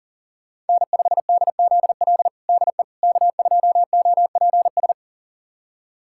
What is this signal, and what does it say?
Morse code, heard as a keyed tone, about 40 wpm, 700 Hz. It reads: N5DZL DE K2OPS
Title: USA Callsign DE Callsign - 5 Characters - 40wpm